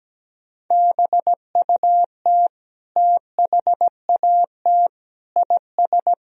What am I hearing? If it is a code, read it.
Morse code, heard as a keyed tone, about 17 wpm, 700 Hz. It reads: BUT THAT IS
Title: Sentences from Top 1000 Words - Review Part 4 - 17wpm